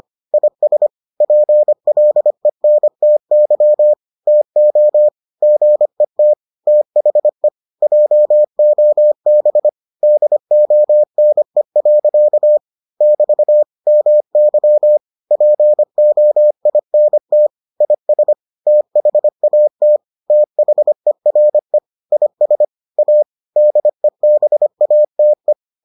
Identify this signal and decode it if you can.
Morse code, heard as a keyed tone, about 25 wpm, 600 Hz. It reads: IS PLENTY TO GET THE JOB DONE. <BT> MY POINT IS THAT THERE IS A DEBATE